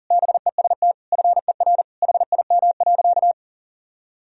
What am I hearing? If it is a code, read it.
Morse code, heard as a keyed tone, about 40 wpm, 700 Hz. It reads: BEST FER HIM.
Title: Sentences from Top 500 Words and Abbreviations 40wpm